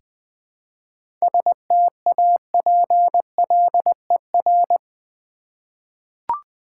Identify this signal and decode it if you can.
Morse code, heard as a keyed tone, about 20 wpm, 700 Hz. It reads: STAPLER
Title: Easy Words with Hints - 20wpm